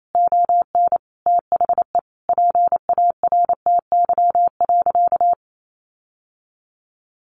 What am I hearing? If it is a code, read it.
Morse code, heard as a keyed tone, about 28 wpm, 700 Hz. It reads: ON THE PARTY.
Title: Sentences from Top 1400 Words 28wpm